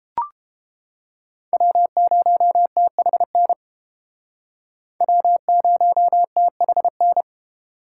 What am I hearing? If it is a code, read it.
Morse code, heard as a keyed tone, about 31 wpm, 700 Hz. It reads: W0THN W0THN